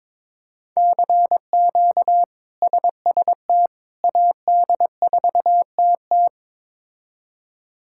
Morse code, heard as a keyed tone, about 22 wpm, 700 Hz. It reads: CQ SST AD4TT
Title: K1USN SST QSO Practice - 22wpm